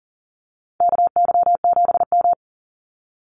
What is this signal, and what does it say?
Morse code, heard as a keyed tone, about 40 wpm, 700 Hz. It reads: KY7M
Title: Call Signs Repeated 3x - 40wpm